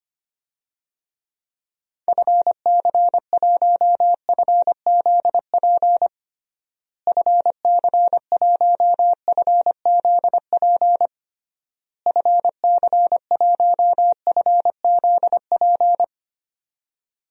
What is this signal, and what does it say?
Morse code, heard as a keyed tone, about 25 wpm, 700 Hz. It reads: FC1FZP FC1FZP FC1FZP
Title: DXCC-100 Call Signs - 6 Characters - Repeated 3x - 25wpm